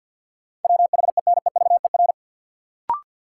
Morse code, heard as a keyed tone, about 52 wpm, 700 Hz. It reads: WHENEVER
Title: Sentences from Top 1200 Words - New Word Drill - 50wpm